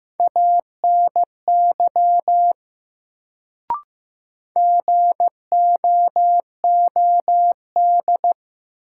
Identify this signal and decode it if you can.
Morse code, heard as a keyed tone, about 15 wpm, 700 Hz. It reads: ANY GOOD
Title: Sentences from Top 200 Words - New Word Drill - 15wpm